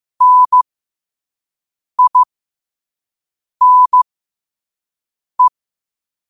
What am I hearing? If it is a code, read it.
Morse code, heard as a keyed tone, about 15 wpm, spaced as for 5 wpm, 1000 Hz. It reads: NINE
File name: qotdfiles / 5wpm